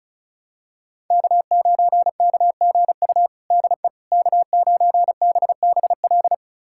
Morse code, heard as a keyed tone, about 36 wpm, 700 Hz. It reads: K9KGU DE K9BBL
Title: USA Callsign DE Callsign - 5 Characters - 35wpm